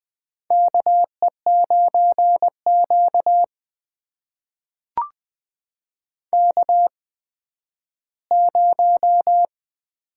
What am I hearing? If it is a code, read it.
Morse code, heard as a keyed tone, about 20 wpm, spaced as for 12 wpm, 700 Hz. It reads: KE9Q K0
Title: USA Call Signs - 8x Character Spacing - 20wpm